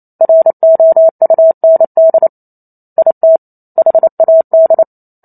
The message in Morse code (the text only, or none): ROUND IT HAD